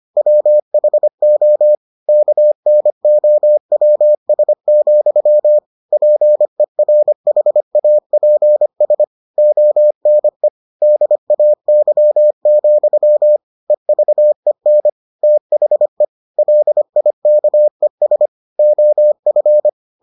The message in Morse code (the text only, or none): WHO KNOWS, PERHAPS ONE DAY, EVEN THE LIKES OF